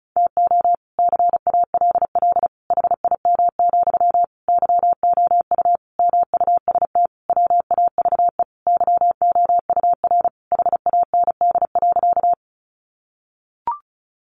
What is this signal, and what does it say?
Morse code, heard as a keyed tone, about 34 wpm, 700 Hz. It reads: TO CALL HIM, YOU MUST WAVE YOUR HAND.
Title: Sentences from Top 1400 Words 35wpm